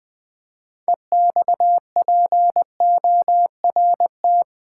Morse code, heard as a keyed tone, about 20 wpm, 700 Hz. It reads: EXPORT
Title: Build A Word - 3 Letters At A Time - 6-Letter Words - 20wpm